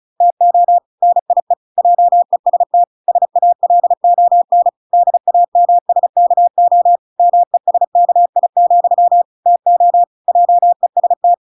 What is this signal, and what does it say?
Morse code, heard as a keyed tone, about 34 wpm, 700 Hz. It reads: TO NIE JEST SALON DAMSKO MESKI, TO JEST